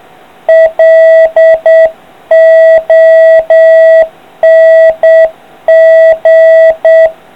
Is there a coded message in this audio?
Yes: Morse code from 0.5 to 7.1 s, heard as a keyed tone, about 8 wpm, 650 Hz. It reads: LONG